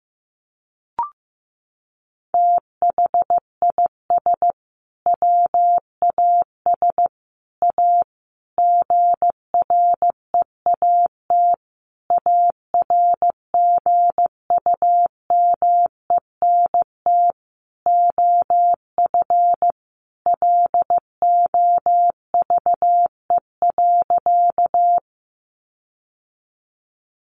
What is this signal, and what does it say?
Morse code, heard as a keyed tone, about 15 wpm, 700 Hz. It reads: THIS WAS A GREAT ARGUMENT OF LOVE.
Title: Sentences from Top 1400 Words 15wpm